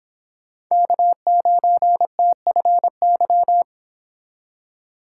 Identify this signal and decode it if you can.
Morse code, heard as a keyed tone, about 26 wpm, 700 Hz. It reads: K9TFY